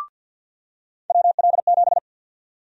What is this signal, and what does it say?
Morse code, heard as a keyed tone, about 47 wpm, 700 Hz. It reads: WL7
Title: USA Call Sign Prefixes - 50wpm